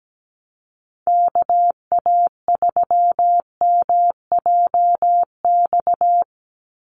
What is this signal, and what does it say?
Morse code, heard as a keyed tone, about 17 wpm, 700 Hz. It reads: KA3MJX
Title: USA Call Signs 17wpm